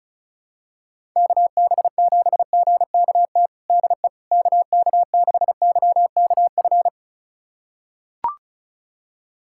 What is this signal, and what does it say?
Morse code, heard as a keyed tone, about 36 wpm, 700 Hz. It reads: KB7GKT DE KK6YKF
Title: USA Callsign DE Callsign 35wpm